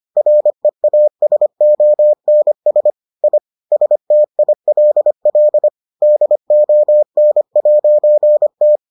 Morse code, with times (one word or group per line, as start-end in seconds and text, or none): 0.2-2.9 REASONS
3.2-3.4 I
3.7-5.7 STILL
6.0-8.8 DON'T